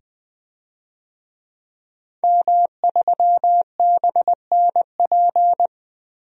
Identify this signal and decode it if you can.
Morse code, heard as a keyed tone, about 20 wpm, 700 Hz. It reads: M3BNP
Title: DXCC-100 Call Signs - 5 Characters - 20wpm